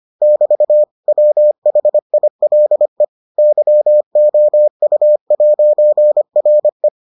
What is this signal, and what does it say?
Morse code, heard as a keyed tone, about 25 wpm, 600 Hz. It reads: <BT> WHILE YOU'RE